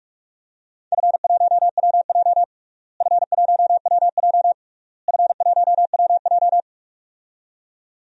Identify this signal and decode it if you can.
Morse code, heard as a keyed tone, about 46 wpm, 700 Hz. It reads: F1WJ F1WJ F1WJ
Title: DXCC-100 Call Signs - 4 Characters - Repeated 3x - 45wpm